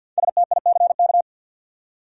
Morse code, heard as a keyed tone, about 47 wpm, 700 Hz. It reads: STICK
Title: Five Letter Words - 50wpm